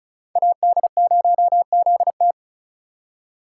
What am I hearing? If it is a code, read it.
Morse code, heard as a keyed tone, about 36 wpm, 700 Hz. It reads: AD0ZT
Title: USA Call Signs - 5 Characters - 2x Character Spacing - 35wpm